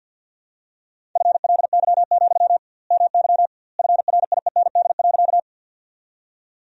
Morse code, heard as a keyed tone, about 48 wpm, 700 Hz. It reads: FLY, MY FRIEND.
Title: 2x Word-Spacing - Sentences from Top 500 Words - 50wpm